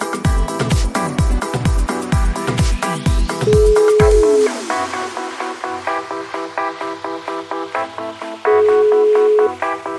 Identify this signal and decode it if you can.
Ringback tone (Europe, 425 Hz), with noise.